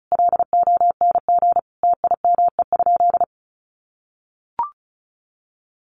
Morse code, heard as a keyed tone, about 34 wpm, 700 Hz. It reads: LONG TIME?
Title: Sentences from Top 1300 Words 35wpm